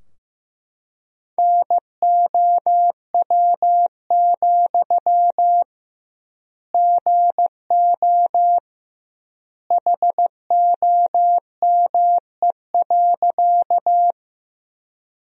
Morse code, heard as a keyed tone, about 15 wpm, 700 Hz. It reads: NOW, GO HOME.
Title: 2x Word-Spacing - Sentences from Top 200 Words - 15wpm